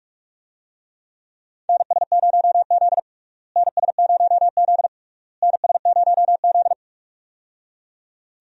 Morse code, heard as a keyed tone, about 41 wpm, 700 Hz. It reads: NS0Z NS0Z NS0Z
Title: USA Call Signs - 4 Characters - Repeated 3x - 45wpm